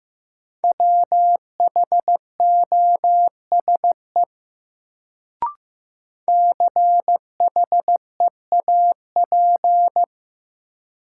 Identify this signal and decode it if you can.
Morse code, heard as a keyed tone, about 15 wpm, 700 Hz. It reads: WHOSE CHEAP
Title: Sentences from Top 1200 Words - New Word Drill - 15wpm